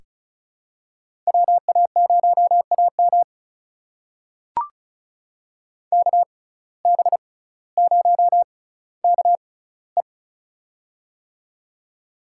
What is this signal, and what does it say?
Morse code, heard as a keyed tone, about 36 wpm, spaced as for 18 wpm, 700 Hz. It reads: WA0AM KB0KE
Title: USA Call Signs - 5 Characters - 6x Character Spacing - 35wpm